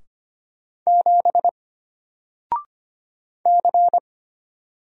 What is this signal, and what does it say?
Morse code, heard as a keyed tone, about 25 wpm, 700 Hz. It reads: 7 C
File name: Single-Letter-Number-25wpm